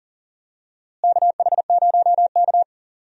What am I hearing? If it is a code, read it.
Morse code, heard as a keyed tone, about 40 wpm, 700 Hz. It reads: KH0K
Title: USA Call Signs - 4 Characters - Repeated 2x - 40wpm